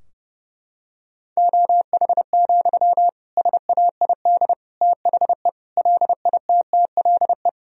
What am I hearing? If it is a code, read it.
Morse code, heard as a keyed tone, about 32 wpm, 700 Hz. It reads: OH, SAID THE LITTLE
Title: Sentences from Top 1000 Words - Review Part 2 - 30wpm